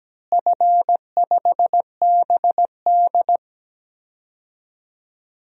Morse code, heard as a keyed tone, about 17 wpm, 700 Hz. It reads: F5BD
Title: DXCC-100 Call Signs - 4 Characters - Repeated 3x - 17wpm